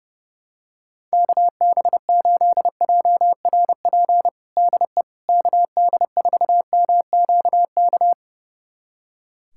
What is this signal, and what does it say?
Morse code, heard as a keyed tone, about 30 wpm, 700 Hz. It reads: KB8JRP DE KD4MQK
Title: USA Callsign DE Callsign - 6 Characters - 30wpm